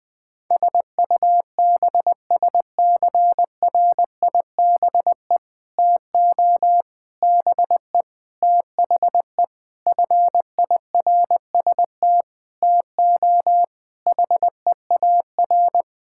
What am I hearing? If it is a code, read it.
Morse code, heard as a keyed tone, about 20 wpm, 700 Hz. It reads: SUBSCRIBE TO BE THE FIRST TO HEAR
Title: Sentences with 6-Word Phrases - 20wpm